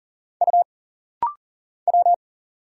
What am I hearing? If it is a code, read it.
Morse code, heard as a keyed tone, about 40 wpm, 700 Hz. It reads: U W
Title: Single Letters - Rapid-Fire 40wpm